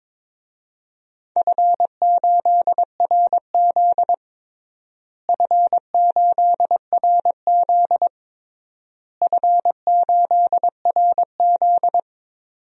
Morse code, heard as a keyed tone, about 22 wpm, 700 Hz. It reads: F8RZ F8RZ F8RZ
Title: DXCC-100 Call Signs - 4 Characters - Repeated 3x - 22wpm